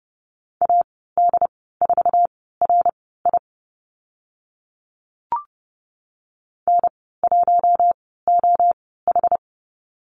Morse code, heard as a keyed tone, about 30 wpm, spaced as for 17 wpm, 700 Hz. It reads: AD4RI N1OH
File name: Lesson 14g - Number 2 - Call Signs 30wpm